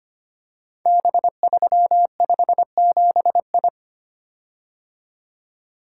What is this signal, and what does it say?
Morse code, heard as a keyed tone, about 25 wpm, 700 Hz. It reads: B357I
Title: Code Groups - 5 Letter:Numbers - 25wpm